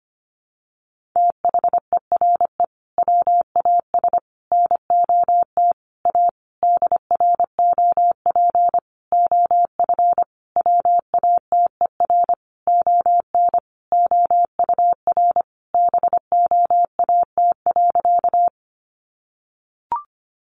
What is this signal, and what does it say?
Morse code, heard as a keyed tone, about 25 wpm, 700 Hz. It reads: THERE WAS NOT A DROP OF WATER ON OUR BOAT.
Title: Sentences from Top 1400 Words 25wpm